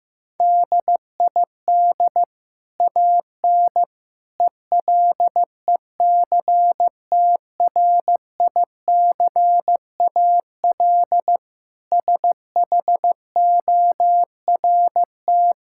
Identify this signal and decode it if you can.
Morse code, heard as a keyed tone, about 16 wpm, 700 Hz. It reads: DID AN ELECTRICAL SHORT
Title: Sentences from Top 1000 Words - Review Part 6 - 15wpm